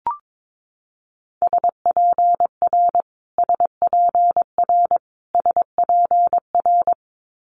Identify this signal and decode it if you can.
Morse code, heard as a keyed tone, about 22 wpm, 700 Hz. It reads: SPR SPR SPR
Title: Multi-letter Phonemes Repeated 3x 22wpm